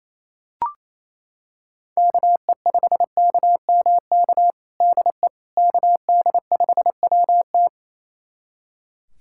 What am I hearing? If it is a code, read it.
Morse code, heard as a keyed tone, about 28 wpm, 700 Hz. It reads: KE5KMK DE KD5WT
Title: USA Callsign DE Callsign 28wpm